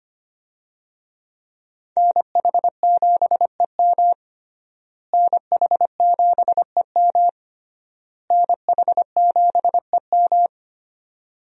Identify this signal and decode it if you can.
Morse code, heard as a keyed tone, about 25 wpm, 700 Hz. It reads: NH7EM NH7EM NH7EM
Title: USA Call Signs - 5 Characters - Repeated 3x - 25wpm